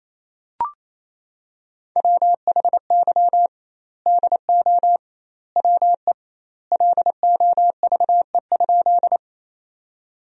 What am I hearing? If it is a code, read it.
Morse code, heard as a keyed tone, about 28 wpm, 700 Hz. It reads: WHY DO WE LOVE?
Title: 2x Word-Spacing - Sentences from Top 400 Words - 28wpm